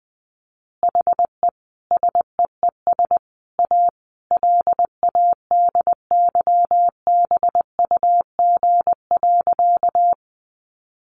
Morse code, heard as a keyed tone, about 20 wpm, 700 Hz. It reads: HE SEES A LADYBUG.
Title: Additional Easy Phrases 20wpm